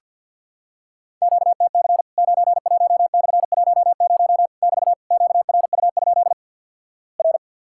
Morse code, heard as a keyed tone, about 50 wpm, 700 Hz. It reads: QTC 91/10 <BT> QRU? R
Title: Worked All Europe DX Contest - QTC Exchange - 50wpm